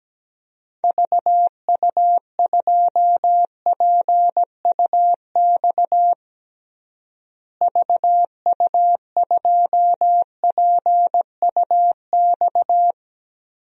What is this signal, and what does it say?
Morse code, heard as a keyed tone, about 17 wpm, 700 Hz. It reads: VU2PUX VU2PUX
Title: DXCC-100 Call Signs - 6 Characters - Repeated 3x - 17wpm